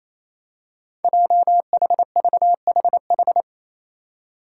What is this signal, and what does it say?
Morse code, heard as a keyed tone, about 28 wpm, 700 Hz. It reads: JHVHH
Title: Five Challenging Letter-Number Combinations 28wpm